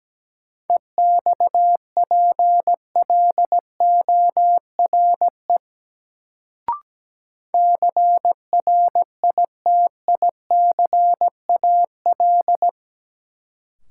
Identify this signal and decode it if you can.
Morse code, heard as a keyed tone, about 17 wpm, 700 Hz. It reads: EXPLORE CRITICAL
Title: Sentences from Top 1100 Words - New Word Drill - 17wpm